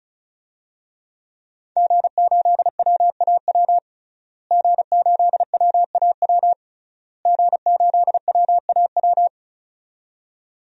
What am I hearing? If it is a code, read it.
Morse code, heard as a keyed tone, about 34 wpm, 690 Hz. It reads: G8WAW G8WAW G8WAW
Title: DXCC-100 Call Signs - 5 Characters - Repeated 3x - 35wpm